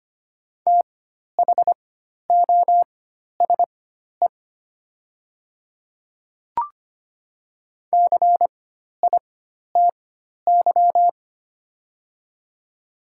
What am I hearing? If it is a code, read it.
Morse code, heard as a keyed tone, about 25 wpm, spaced as for 12 wpm, 700 Hz. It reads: THOSE CITY
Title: 4x Character Spacing - Top 300 Words - 25wpm